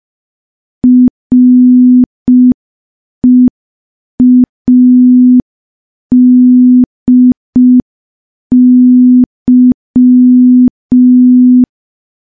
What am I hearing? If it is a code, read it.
Morse code, heard as a keyed tone, about 5 wpm, 250 Hz. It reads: READY
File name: morse / morse-training-05WPM